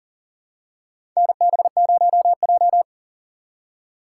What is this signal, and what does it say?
Morse code, heard as a keyed tone, about 40 wpm, 700 Hz. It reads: NB0J
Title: USA Call Signs - 4 Characters - 6x Character Spacing - 40wpm